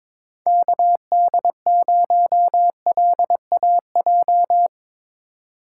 Morse code, heard as a keyed tone, about 23 wpm, 700 Hz. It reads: KD0LAJ
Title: USA Call Signs - 6 Characters - 3x Character Spacing - 22wpm